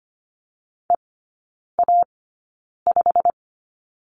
Morse code, heard as a keyed tone, about 25 wpm, spaced as for 9 wpm, 700 Hz. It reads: EA5
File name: Lesson 15e - Number 5 - Three Characters 25wpm